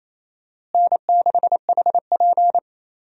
Morse code, heard as a keyed tone, about 28 wpm, 700 Hz. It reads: N6HP
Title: USA Call Signs - 4 Characters - Repeated 2x - 28wpm